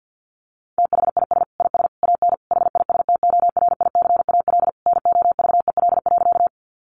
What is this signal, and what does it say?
Morse code, heard as a keyed tone, about 50 wpm, 700 Hz. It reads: THIS IS AN HISTORICAL NOVEL.
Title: Sentences from Top 1300 Words 50wpm